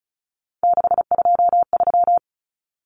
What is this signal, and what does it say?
Morse code, heard as a keyed tone, about 34 wpm, 700 Hz. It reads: 623
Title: Common Cut Numbers - 3-Digit 35wpm